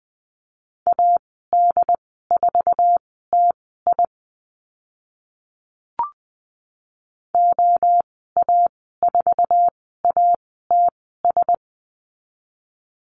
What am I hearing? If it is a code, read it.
Morse code, heard as a keyed tone, about 20 wpm, spaced as for 15 wpm, 700 Hz. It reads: AD4TI OA4ATS
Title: 082 Lesson 14g - Number 2 - Call Signs 20wpm